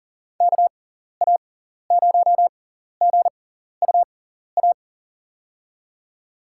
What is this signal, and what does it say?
Morse code, heard as a keyed tone, about 40 wpm, spaced as for 14 wpm, 700 Hz. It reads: KA0GUA
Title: USA Call Signs - 6 Characters - 6x Character Spacing - 40wpm